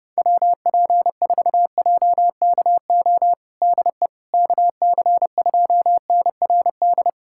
Morse code, heard as a keyed tone, about 32 wpm, 700 Hz. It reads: WP4JKO DE KC2NRD
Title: USA Callsign DE Callsign 30wpm